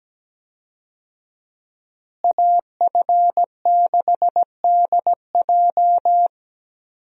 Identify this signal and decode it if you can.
Morse code, heard as a keyed tone, about 17 wpm, 700 Hz. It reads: AF6DJ
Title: USA Call Signs - 5 Characters - Repeated 3x - 17wpm